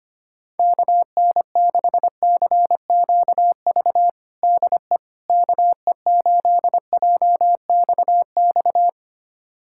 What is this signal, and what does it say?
Morse code, heard as a keyed tone, about 25 wpm, 700 Hz. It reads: KN6CQV DE KE8JXX
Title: USA Callsign DE Callsign - 6 Characters - 25wpm